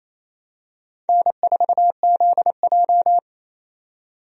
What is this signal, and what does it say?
Morse code, heard as a keyed tone, about 28 wpm, 690 Hz. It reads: N4ZJ
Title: USA Call Signs - 4 Characters - 28wpm